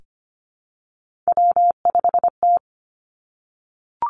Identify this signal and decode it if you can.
Morse code, heard as a keyed tone, about 25 wpm, 700 Hz. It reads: W5T
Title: Three Challenging Letter-Number Combinations 25wpm